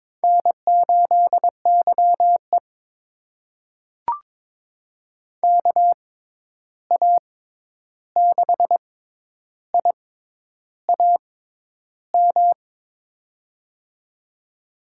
Morse code, heard as a keyed tone, about 22 wpm, spaced as for 10 wpm, 700 Hz. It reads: N8YE KA6IAM
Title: USA Call Signs - 6x Character Spacing - 22wpm